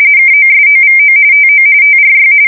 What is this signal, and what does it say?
RTTY, about 45.45 baud. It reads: KOSOVO TO TRY
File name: fsk170 / usnavy5